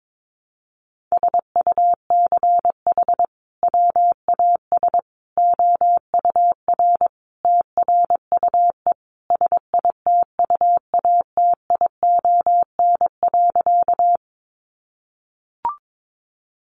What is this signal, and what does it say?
Morse code, heard as a keyed tone, about 22 wpm, 700 Hz. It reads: SUCH WAS OUR TRUE SITUATION.
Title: Sentences from Top 1200 Words 22wpm